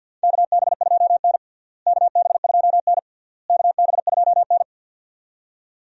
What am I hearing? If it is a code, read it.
Morse code, heard as a keyed tone, about 48 wpm, 700 Hz. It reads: KB2N KB2N KB2N
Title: USA Call Signs - 4 Characters - Repeated 3x - 50wpm